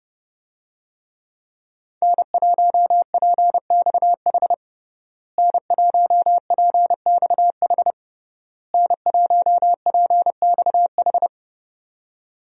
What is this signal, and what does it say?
Morse code, heard as a keyed tone, about 32 wpm, 700 Hz. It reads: N1PXH N1PXH N1PXH
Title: USA Call Signs - 5 Characters - Repeated 3x - 30wpm